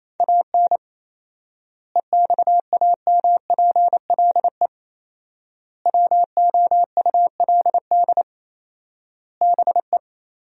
Morse code, heard as a keyed tone, about 28 wpm, 700 Hz. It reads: AN EXAMPLE WOULD BE
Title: 4x Word-Spacing - Sentences from Top 400 Words - 28wpm